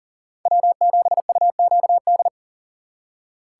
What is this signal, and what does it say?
Morse code, heard as a keyed tone, about 51 wpm, 700 Hz. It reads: W7UQD